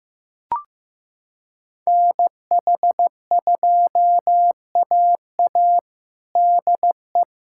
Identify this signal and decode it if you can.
Morse code, heard as a keyed tone, about 15 wpm, 700 Hz. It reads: NH2AA DE
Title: USA Callsign DE Callsign - 5 Characters - 15wpm